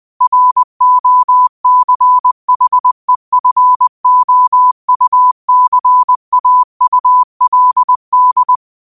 Morse code, heard as a keyed tone, about 20 wpm, 1000 Hz. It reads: ROCHEFOUCAULD
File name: qotdfiles / 20wpm